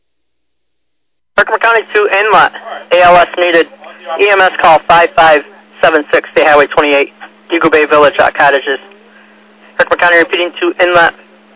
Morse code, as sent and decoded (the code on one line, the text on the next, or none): none